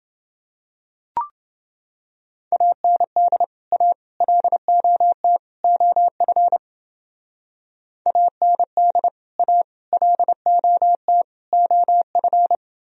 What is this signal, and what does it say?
Morse code, heard as a keyed tone, about 29 wpm, 700 Hz. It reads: AND A LOT OF AND A LOT OF